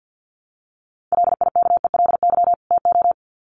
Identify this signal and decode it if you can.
Morse code, heard as a keyed tone, about 48 wpm, 700 Hz. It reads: LIKELY TO
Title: Sets of 2 Words 50wpm